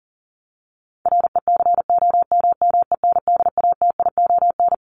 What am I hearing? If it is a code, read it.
Morse code, heard as a keyed tone, about 40 wpm, 700 Hz. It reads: RECOMMENDATION
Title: Top 1251-1500 Words 40wpm